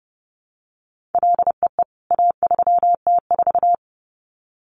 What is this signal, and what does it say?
Morse code, heard as a keyed tone, about 30 wpm, 700 Hz. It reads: LEE A3T4
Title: CWT Exchange 30wpm